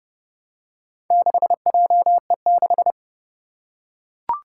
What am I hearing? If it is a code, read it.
Morse code, heard as a keyed tone, about 30 wpm, 700 Hz. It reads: 6JE6
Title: Four Challenging Letter-Number Combinations 30wpm